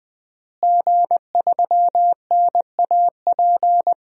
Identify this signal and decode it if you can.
Morse code, heard as a keyed tone, about 20 wpm, 700 Hz. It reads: G3NAP
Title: DXCC-100 Call Signs - 5 Characters - Repeated 3x - 20wpm